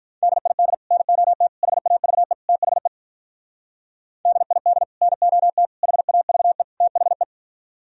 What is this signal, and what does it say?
Morse code, heard as a keyed tone, about 47 wpm, 700 Hz. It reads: DID NOT HAVE THE DID NOT HAVE THE